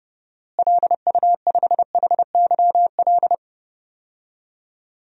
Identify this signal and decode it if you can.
Morse code, heard as a keyed tone, about 30 wpm, 700 Hz. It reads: LU5HYL
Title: DXCC-100 Call Signs - 6 Characters - Repeated 2x - 30wpm